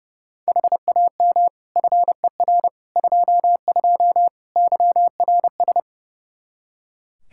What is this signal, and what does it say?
Morse code, heard as a keyed tone, about 30 wpm, 700 Hz. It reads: HAM FER 22 YRS
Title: Practice QSO - Exchange 3 - More Info - 30wpm